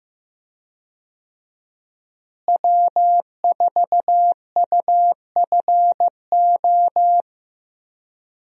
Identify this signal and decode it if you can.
Morse code, heard as a keyed tone, about 15 wpm, 700 Hz. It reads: W4UFO
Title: USA Call Signs - 5 Characters - Repeated 2x - 15wpm